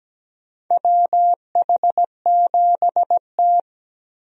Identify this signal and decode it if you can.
Morse code, heard as a keyed tone, about 17 wpm, 700 Hz. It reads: WH7T
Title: USA Call Signs - 4 Characters - Repeated 2x - 17wpm